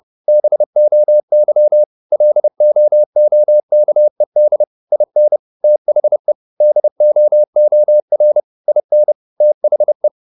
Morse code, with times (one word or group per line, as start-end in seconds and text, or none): 0.3-1.8 BOY
2.1-4.6 LOOKED
4.9-5.4 IN
5.6-6.3 THE
6.6-8.4 DOOR
8.7-9.1 IN
9.4-10.1 THE